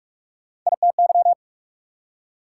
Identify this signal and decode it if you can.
Morse code, heard as a keyed tone, about 44 wpm, 690 Hz. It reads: ITY
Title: Prefixes and Suffixes 45wpm